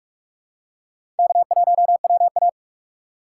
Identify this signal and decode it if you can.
Morse code, heard as a keyed tone, about 46 wpm, 700 Hz. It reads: K1WA
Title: USA Call Signs - 4x Character Spacing - 45wpm